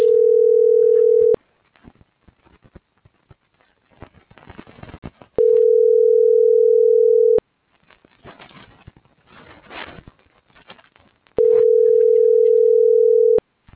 Ringback tone (North America), 440 + 480 Hz.